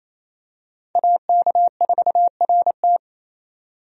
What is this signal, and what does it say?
Morse code, heard as a keyed tone, about 27 wpm, 700 Hz. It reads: AK4RT
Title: USA Call Signs - 5 Characters - 28wpm